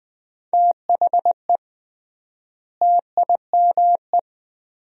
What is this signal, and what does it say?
Morse code, heard as a keyed tone, about 20 wpm, 700 Hz. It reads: THE TIME
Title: 3x Word-Spacing - Sentences from Top 100 Words - 20wpm